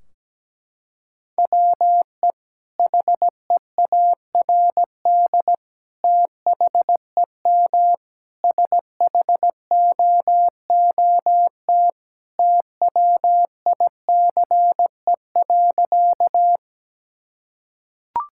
Morse code, heard as a keyed tone, about 17 wpm, 700 Hz. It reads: WE HEARD THEM SHOOT TWICE.
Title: Sentences from Top 1600 Words 17wpm